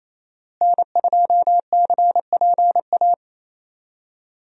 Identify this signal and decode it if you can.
Morse code, heard as a keyed tone, about 28 wpm, 700 Hz. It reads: N2CPA
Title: USA Call Signs - 5 Characters - Repeated 3x - 28wpm